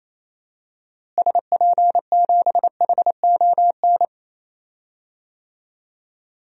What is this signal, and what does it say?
Morse code, heard as a keyed tone, about 28 wpm, 700 Hz. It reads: SP7HON
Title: DXCC-100 Call Signs - 6 Characters - Repeated 2x - 28wpm